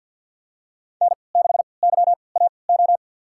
Morse code, heard as a keyed tone, about 47 wpm, spaced as for 28 wpm, 700 Hz. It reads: N6YAO
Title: USA Call Signs - 3x Character Spacing - 50wpm